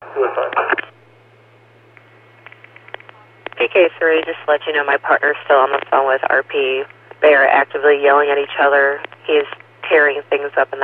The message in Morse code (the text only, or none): none